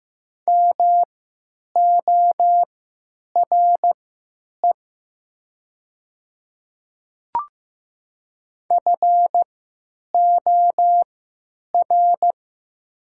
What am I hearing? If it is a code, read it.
Morse code, heard as a keyed tone, about 15 wpm, spaced as for 9 wpm, 700 Hz. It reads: MORE FOR
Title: 3x Character Spacing - Top 100 Words - 15wpm